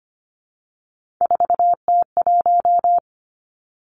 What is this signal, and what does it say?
Morse code, heard as a keyed tone, about 25 wpm, 700 Hz. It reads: 4T1
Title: Three Challenging Letter-Number Combinations 25wpm